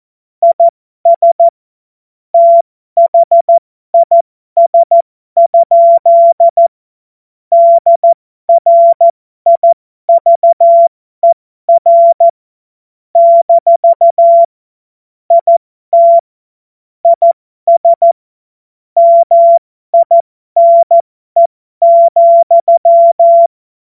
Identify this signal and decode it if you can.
Morse code, heard as a keyed tone, about 14 wpm, spaced as for 12 wpm, 680 Hz. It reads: IS THIS? DRIVER - IT IS MINE,